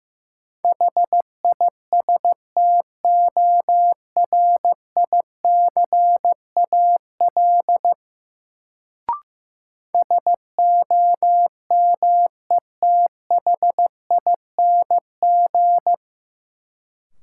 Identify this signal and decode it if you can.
Morse code, heard as a keyed tone, about 15 wpm, 700 Hz. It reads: HISTORICAL SOMETHING
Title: Sentences from Top 600 Words - New Word Drill - 15wpm